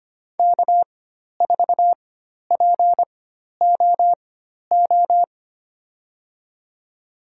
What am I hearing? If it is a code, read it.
Morse code, heard as a keyed tone, about 25 wpm, spaced as for 12 wpm, 700 Hz. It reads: K4POO
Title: USA Call Signs - 5 Characters - 4x Character Spacing - 25wpm